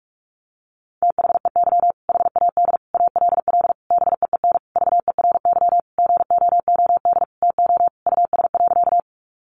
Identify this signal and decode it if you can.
Morse code, heard as a keyed tone, about 44 wpm, 700 Hz. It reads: THEY HAD ALL BEEN VERY GOOD TO US.
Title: Sentences from Top 200 Words 45wpm